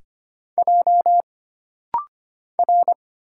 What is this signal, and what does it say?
Morse code, heard as a keyed tone, about 25 wpm, 700 Hz. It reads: J R
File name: Single Character (Letter:Number:Punctuation:Prosign) - Rapid-Fire - 25wpm